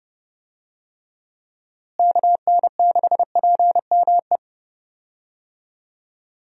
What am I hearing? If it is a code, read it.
Morse code, heard as a keyed tone, about 30 wpm, 700 Hz. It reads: KN6PME
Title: USA Call Signs - 6 Characters - Repeated 2x - 30wpm